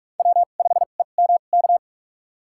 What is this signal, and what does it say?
Morse code, heard as a keyed tone, about 46 wpm, spaced as for 33 wpm, 700 Hz. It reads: W5EMK
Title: USA Call Signs - 5 Characters - 2x Character Spacing - 45wpm